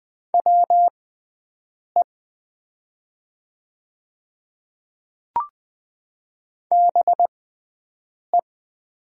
Morse code, heard as a keyed tone, about 20 wpm, spaced as for 7 wpm, 700 Hz. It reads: WE BE
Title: 6x Character Spacing - Top 100 Words - 20wpm